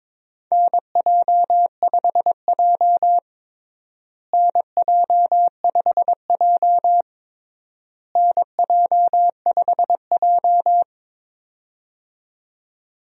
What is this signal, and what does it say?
Morse code, heard as a keyed tone, about 22 wpm, 700 Hz. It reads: NJ5J NJ5J NJ5J
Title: USA Call Signs - 4 Characters - Repeated 3x - 22wpm